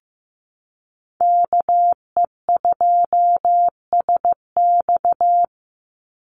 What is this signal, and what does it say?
Morse code, heard as a keyed tone, about 15 wpm, 700 Hz. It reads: KE2SX
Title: Call Signs Repeated 2x - 15wpm